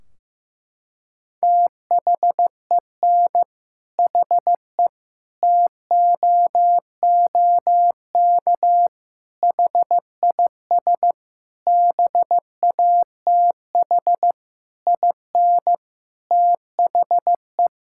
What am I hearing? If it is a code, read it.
Morse code, heard as a keyed tone, about 15 wpm, 700 Hz. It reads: THEN HE TOOK HIS BATH IN THE
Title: Sentences from Top 1600 Words 15wpm